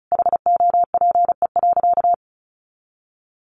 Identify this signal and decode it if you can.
Morse code, heard as a keyed tone, about 34 wpm, 700 Hz. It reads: HOPE.
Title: Sentences from Top 1000 Words 35wpm